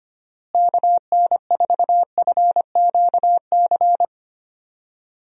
Morse code, heard as a keyed tone, about 25 wpm, 700 Hz. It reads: KN4FQC
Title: USA Call Signs - 6 Characters - Repeated 2x - 25wpm